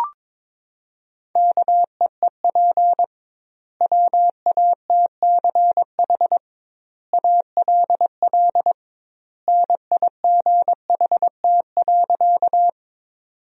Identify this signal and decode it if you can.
Morse code, heard as a keyed tone, about 22 wpm, 700 Hz. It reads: KEEP WATCH ALL NIGHT.
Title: 2x Word-Spacing - Sentences from Top 300 Words - 22wpm